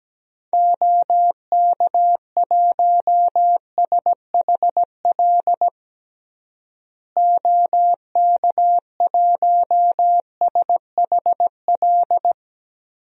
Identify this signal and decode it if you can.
Morse code, heard as a keyed tone, about 17 wpm, 700 Hz. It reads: OK1SHL OK1SHL
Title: DXCC-100 Call Signs - 6 Characters - Repeated 3x - 17wpm